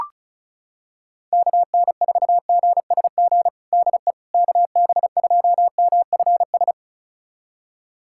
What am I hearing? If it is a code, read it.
Morse code, heard as a keyed tone, about 34 wpm, 700 Hz. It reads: KN4GSG DE KB2MFS
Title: USA Callsign DE Callsign - 6 Characters - 35wpm